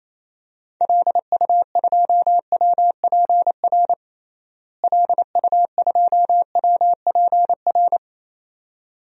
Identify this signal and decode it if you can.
Morse code, heard as a keyed tone, about 28 wpm, 700 Hz. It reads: LU2WPR LU2WPR
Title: DXCC-100 Call Signs - 6 Characters - Repeated 2x - 28wpm